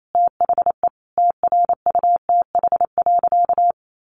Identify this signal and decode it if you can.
Morse code, heard as a keyed tone, about 28 wpm, 700 Hz. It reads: THE TRUTH.
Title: Sentences from Top 1300 Words 28wpm